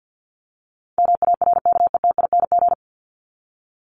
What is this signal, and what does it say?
Morse code, heard as a keyed tone, about 50 wpm, 700 Hz. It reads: MARKETING
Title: ING Suffix - Top 100 Words with Base Words 50wpm